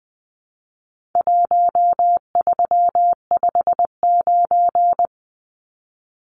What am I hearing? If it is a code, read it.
Morse code, heard as a keyed tone, about 20 wpm, 700 Hz. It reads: 1359
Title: Numbers - Contest Style - 4-Digit 20wpm